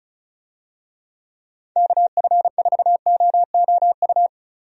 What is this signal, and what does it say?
Morse code, heard as a keyed tone, about 34 wpm, 700 Hz. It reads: KF4OOU
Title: USA Call Signs - 6 Characters - Repeated 2x - 35wpm